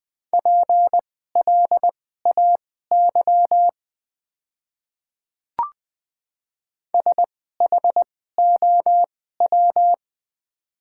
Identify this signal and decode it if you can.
Morse code, heard as a keyed tone, about 20 wpm, spaced as for 14 wpm, 690 Hz. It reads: PLAY SHOW
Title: 2x Character Spacing - Top 200 Words - 20wpm